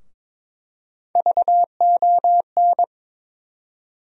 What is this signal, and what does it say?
Morse code, heard as a keyed tone, about 21 wpm, 700 Hz. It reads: VON
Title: USA Call Sign Postfixes - 3 Letters 22wpm